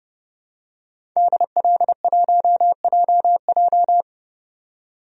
Morse code, heard as a keyed tone, about 30 wpm, 700 Hz. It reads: DL1JJ
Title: DXCC-100 Call Signs - 5 Characters - 30wpm